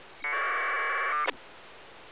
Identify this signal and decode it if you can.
Packet radio (AX.25, as in APRS).